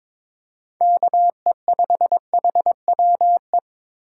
Morse code, heard as a keyed tone, about 22 wpm, 700 Hz. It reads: KE5HWE
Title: USA Call Signs - 6 Characters - 2x Character Spacing - 22wpm